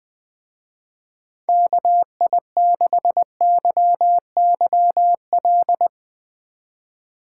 Morse code, heard as a keyed tone, about 22 wpm, 700 Hz. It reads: KI6YYL
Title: Speed Racing - USA 2x3 Call Signs 20wpm